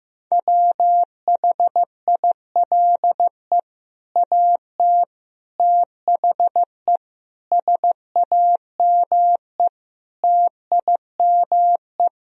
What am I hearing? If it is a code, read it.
Morse code, heard as a keyed tone, about 16 wpm, 700 Hz. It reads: WHILE AT THE SAME TIME
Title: Speed-Racing - Sets of 5 Words 15wpm